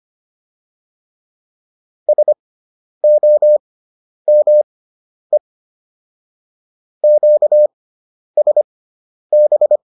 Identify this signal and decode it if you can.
Morse code, heard as a keyed tone, about 25 wpm, spaced as for 10 wpm, 600 Hz. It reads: SOME QSB